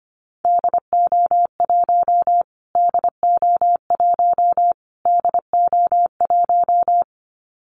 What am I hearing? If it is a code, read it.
Morse code, heard as a keyed tone, about 25 wpm, 700 Hz. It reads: DO1 DO1 DO1
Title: Callsign Prefixes for Top 25 DXCC - Repeated 3x - 25wpm